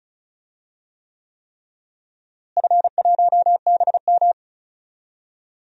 Morse code, heard as a keyed tone, about 36 wpm, 700 Hz. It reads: F1BM
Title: DXCC-100 Call Signs - 4 Characters - Repeated 3x - 35wpm